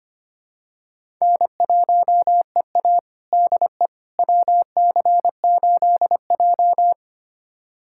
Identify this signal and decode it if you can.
Morse code, heard as a keyed tone, about 25 wpm, 700 Hz. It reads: N1EA DE WC8J
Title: USA Callsign DE Callsign - 4 Characters - 25wpm